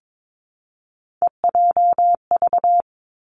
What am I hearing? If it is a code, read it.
Morse code, heard as a keyed tone, about 22 wpm, 700 Hz. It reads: EJV
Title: Three Challenging Letter-Number Combinations 22wpm